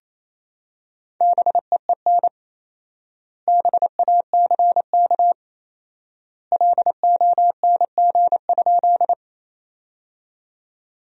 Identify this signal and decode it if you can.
Morse code, heard as a keyed tone, about 28 wpm, 700 Hz. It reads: BEEN BACK LONG?
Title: 4x Word-Spacing - Sentences from Top 200 Words - 28wpm